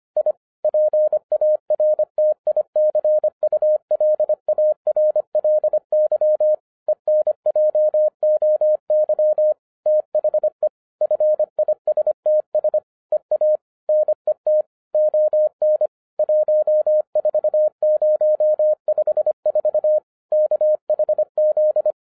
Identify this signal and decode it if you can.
Morse code, heard as a keyed tone, about 25 wpm, 610 Hz. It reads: I PARTICULARLY ENJOY THE FISTS EA NET ON 14054 KHZ